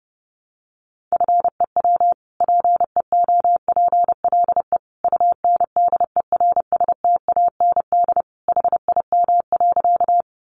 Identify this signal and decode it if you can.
Morse code, heard as a keyed tone, about 30 wpm, 700 Hz. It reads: FEW PEOPLE UNDERSTAND HIM.
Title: Sentences from Top 600 Words 30wpm